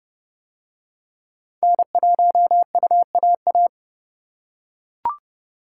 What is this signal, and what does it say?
Morse code, heard as a keyed tone, about 29 wpm, 700 Hz. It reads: N1UAA